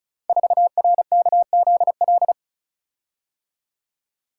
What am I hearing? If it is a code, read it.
Morse code, heard as a keyed tone, about 36 wpm, 700 Hz. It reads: 4RKZL
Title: Code Groups - 5 Letter:Numbers - 35wpm